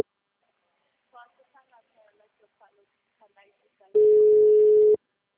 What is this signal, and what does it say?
Ringback tone (Europe, 425 Hz), with speech.